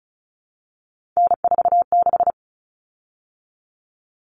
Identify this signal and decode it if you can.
Morse code, heard as a keyed tone, about 34 wpm, 700 Hz. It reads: N46
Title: Common Cut Numbers - 3-Digit 35wpm